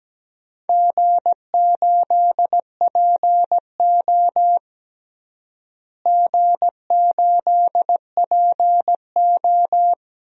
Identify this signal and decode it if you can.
Morse code, heard as a keyed tone, about 17 wpm, 700 Hz. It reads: G8PO G8PO
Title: DXCC-100 Call Signs - 4 Characters - Repeated 3x - 17wpm